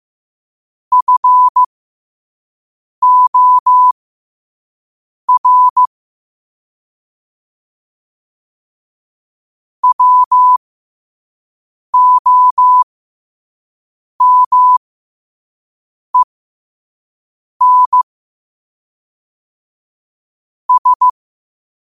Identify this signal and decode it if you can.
Morse code, heard as a keyed tone, about 15 wpm, spaced as for 5 wpm, 1000 Hz. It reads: FOR WOMEN S
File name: qotdfiles / 5wpm